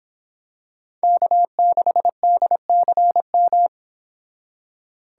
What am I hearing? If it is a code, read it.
Morse code, heard as a keyed tone, about 25 wpm, 700 Hz. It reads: K6DCM